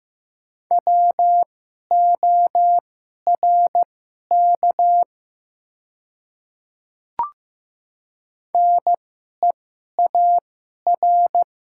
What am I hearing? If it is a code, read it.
Morse code, heard as a keyed tone, about 15 wpm, spaced as for 11 wpm, 700 Hz. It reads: WORK NEAR
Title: 2x Character Spacing - Top 200 Words - 15wpm